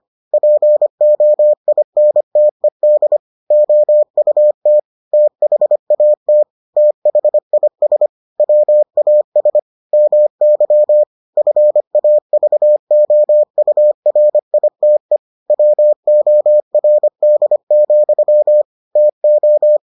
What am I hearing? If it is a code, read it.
Morse code, heard as a keyed tone, about 25 wpm, 600 Hz. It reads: POINTED OUT THAT THIS WAS MY FAVOURITE WORD, TO